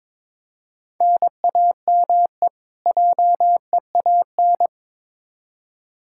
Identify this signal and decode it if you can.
Morse code, heard as a keyed tone, about 22 wpm, 700 Hz. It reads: NAME JEAN
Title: Practice QSO - Exchange 1 - Beginning - 22wpm